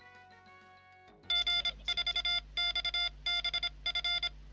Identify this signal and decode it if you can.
Morse code, heard as a keyed tone, about 26 wpm, 2100 Hz, with noise. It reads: G4XBF